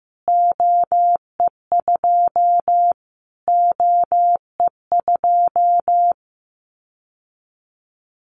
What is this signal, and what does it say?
Morse code, heard as a keyed tone, about 15 wpm, 700 Hz. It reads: OE2 OE2
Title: Callsign Prefixes for Top 25 DXCC - Repeated 3x - 15wpm